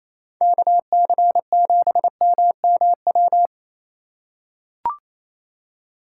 Morse code, heard as a keyed tone, about 28 wpm, 700 Hz. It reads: KC7MMW
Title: USA Call Signs - 6 Characters - 3x Character Spacing - 28wpm